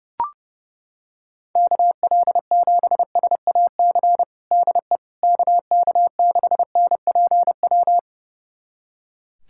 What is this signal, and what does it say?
Morse code, heard as a keyed tone, about 28 wpm, 700 Hz. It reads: KL7SAC DE KK6NPW
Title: USA Callsign DE Callsign - 6 Characters - 30wpm